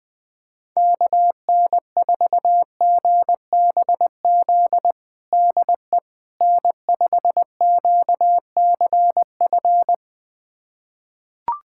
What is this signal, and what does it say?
Morse code, heard as a keyed tone, about 20 wpm, 700 Hz. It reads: KN4GBZ DE N5QCF
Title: USA Callsign DE Callsign 20wpm